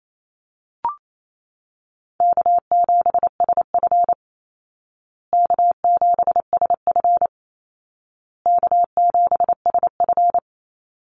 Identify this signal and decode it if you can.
Morse code, heard as a keyed tone, about 28 wpm, 700 Hz. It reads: K7SF K7SF K7SF
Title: Easy Call Signs Repeated 3x - 28wpm